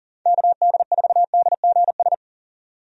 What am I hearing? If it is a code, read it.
Morse code, heard as a keyed tone, about 40 wpm, 700 Hz. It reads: KD4DGS
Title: USA Call Signs - 6 Characters - 40wpm